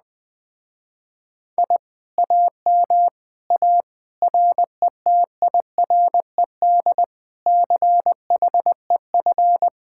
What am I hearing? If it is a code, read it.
Morse code, heard as a keyed tone, about 20 wpm, 700 Hz. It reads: I AM A RETIRED CHEF
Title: Speed-Racing Common QSO Phrases Encore - 17wpm